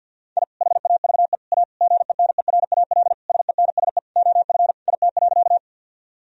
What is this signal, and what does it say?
Morse code, heard as a keyed tone, about 50 wpm, 700 Hz. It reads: I HAVE A GENERAL SENSE OF IT.
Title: Sentences from Top 1000 Words - Review Part 4 - 50wpm